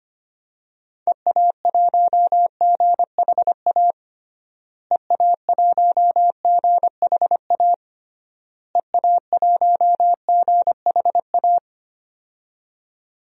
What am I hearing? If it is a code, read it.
Morse code, heard as a keyed tone, about 25 wpm, 700 Hz. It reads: EA1GHA EA1GHA EA1GHA
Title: DXCC-100 Call Signs - 6 Characters - Repeated 3x - 25wpm